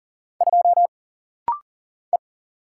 Morse code, heard as a keyed tone, about 39 wpm, 700 Hz. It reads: 2 E
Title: Single Letter-Number - Rapid-Fire 40wpm